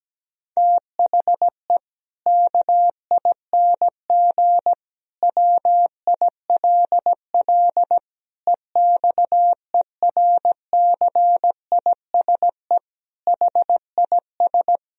Morse code, heard as a keyed tone, about 17 wpm, 700 Hz. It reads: THE KING WILL EXERCISE HIS
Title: Sentences from Top 1000 Words - Review Part 4 - 17wpm